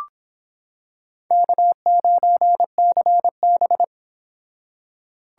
Morse code, heard as a keyed tone, about 26 wpm, 700 Hz. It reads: K9CB